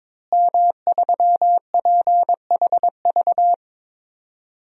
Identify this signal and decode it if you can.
Morse code, heard as a keyed tone, about 22 wpm, 700 Hz. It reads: M3PHV
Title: DXCC-100 Call Signs - 5 Characters - Repeated 2x - 22wpm